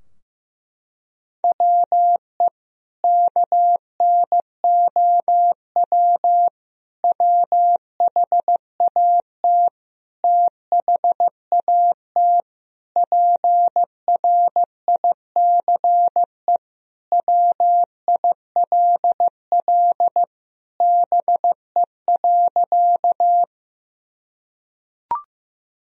Morse code, heard as a keyed tone, about 15 wpm, 700 Hz. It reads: WE KNOW WHAT THAT PRICE WILL BE.
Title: Sentences from Top 1000 Words - Review Part 4 - 15wpm